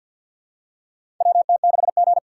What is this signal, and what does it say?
Morse code, heard as a keyed tone, about 47 wpm, 700 Hz. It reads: WT6G
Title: USA Call Signs - 2x Character Spacing - 50wpm